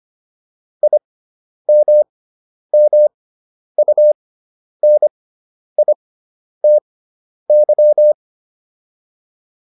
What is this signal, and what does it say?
Morse code, heard as a keyed tone, about 25 wpm, spaced as for 10 wpm, 600 Hz. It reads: IMMUNITY